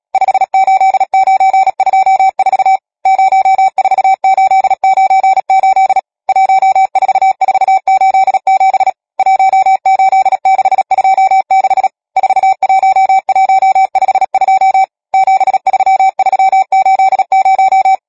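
Morse code, heard as a keyed tone, about 37 wpm, 770 Hz. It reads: 58924 04898 14487 18626 41152 73380
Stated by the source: Morse code number practice, machine-sent training text (Nácvik telegrafie, OK1FCB).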